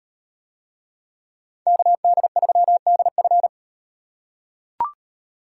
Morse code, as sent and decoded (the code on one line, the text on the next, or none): -.- -.. ...-- -.. ..-.
KD3DF